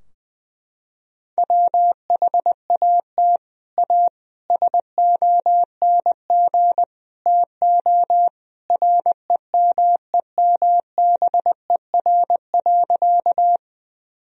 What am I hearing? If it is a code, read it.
Morse code, heard as a keyed tone, about 20 wpm, 700 Hz. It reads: WHAT A SONG TO REMEMBER.
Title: Sentences from Top 1000 Words - Review Part 2 - 20wpm